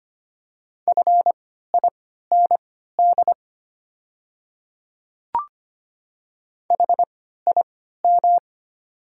Morse code, heard as a keyed tone, about 25 wpm, spaced as for 14 wpm, 690 Hz. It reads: FIND HIM
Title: 3x Character Spacing - Top 500 Words - 25wpm